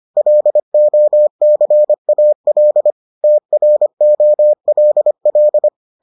Morse code, heard as a keyed tone, about 25 wpm, 600 Hz. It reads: LOCAL TROLL